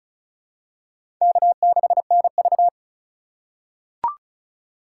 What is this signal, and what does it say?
Morse code, heard as a keyed tone, about 36 wpm, 700 Hz. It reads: K6NV
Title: USA Call Signs - 4 Characters - 2x Character Spacing - 35wpm